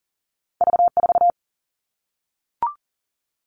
Morse code, heard as a keyed tone, about 40 wpm, 700 Hz. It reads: V 4
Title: Two Challenging Letter-Number Combinations 40wpm